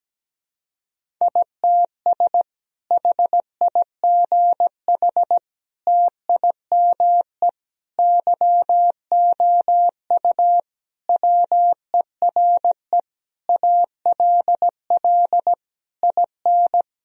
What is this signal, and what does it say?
Morse code, heard as a keyed tone, about 17 wpm, 700 Hz. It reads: ITS HIGH TIME YOU WERE ALL IN
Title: Sentences from Top 1000 Words - Review Part 1 - 17wpm